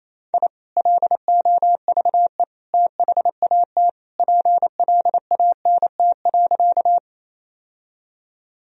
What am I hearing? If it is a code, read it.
Morse code, heard as a keyed tone, about 28 wpm, 700 Hz. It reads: I LOVE THAT PLANT.
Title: Sentences from Top 1000 Words - Review Part 1 - 28wpm